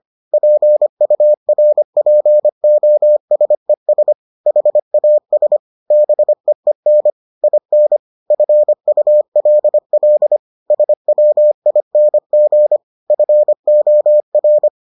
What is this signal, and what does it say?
Morse code, heard as a keyed tone, about 25 wpm, 600 Hz. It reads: PURPOSES HAS BEEN IN FULL SWING FOR